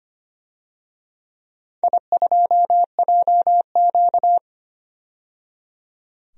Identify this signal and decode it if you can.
Morse code, heard as a keyed tone, about 25 wpm, 700 Hz. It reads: I2JQ
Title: DXCC-100 Call Signs - 4 Characters - 25wpm